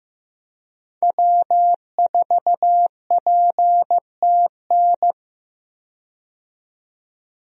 Morse code, heard as a keyed tone, about 15 wpm, 700 Hz. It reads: W4PTN
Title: USA Call Signs - 5 Characters - 15wpm